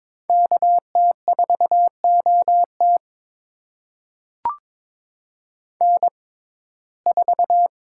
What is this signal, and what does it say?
Morse code, heard as a keyed tone, about 21 wpm, spaced as for 16 wpm, 700 Hz. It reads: KT4OT N4
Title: USA Call Signs - 5 Characters - 6x Character Spacing - 22wpm